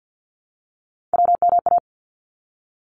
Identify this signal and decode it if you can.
Morse code, heard as a keyed tone, about 50 wpm, 700 Hz. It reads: WMA
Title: ARRL Sections 50wpm